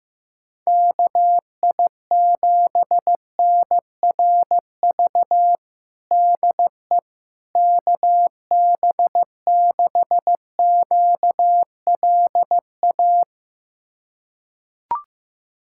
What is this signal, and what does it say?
Morse code, heard as a keyed tone, about 15 wpm, 700 Hz. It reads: KI7NRV DE KB6QLA
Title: USA Callsign DE Callsign 15wpm